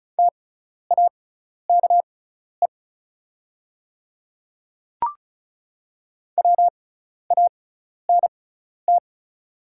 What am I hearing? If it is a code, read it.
Morse code, heard as a keyed tone, about 36 wpm, spaced as for 12 wpm, 700 Hz. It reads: TAKE WANT
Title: 6x Character Spacing - Top 200 Words - 35wpm